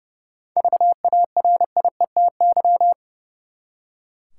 Morse code, heard as a keyed tone, about 30 wpm, 700 Hz. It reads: VARIETY
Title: Seven Letter Words - 30wpm